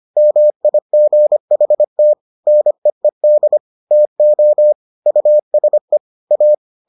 Morse code, heard as a keyed tone, about 25 wpm, 600 Hz. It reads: MIGHT NEED TO USE A